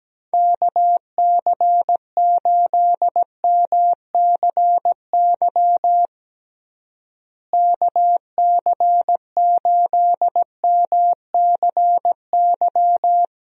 Morse code, heard as a keyed tone, about 17 wpm, 700 Hz. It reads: KC8MCY KC8MCY
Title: USA Call Signs - 6 Characters - Repeated 3x - 17wpm